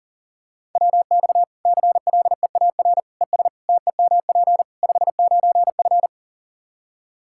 Morse code, heard as a keyed tone, about 37 wpm, 700 Hz. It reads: WX CLEAR ES TEMP 59F
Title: Practice QSO - Exchange 2 - Info - 40wpm